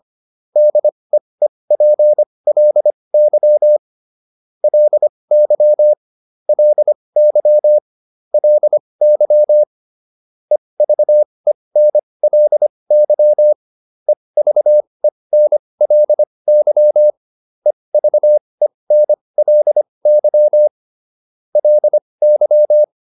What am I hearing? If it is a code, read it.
Morse code, heard as a keyed tone, about 25 wpm, spaced as for 20 wpm, 600 Hz. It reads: DEEPLY LY LY LY EVENLY EVENLY EVENLY LY